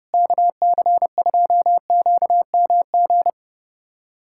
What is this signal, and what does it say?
Morse code, heard as a keyed tone, about 30 wpm, 700 Hz. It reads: KC2QMG
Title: USA Callsign DE Callsign - 6 Characters - 30wpm